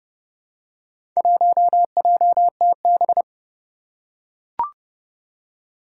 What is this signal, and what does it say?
Morse code, heard as a keyed tone, about 30 wpm, 700 Hz. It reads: 1JTB
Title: Four Challenging Letter-Number Combinations 30wpm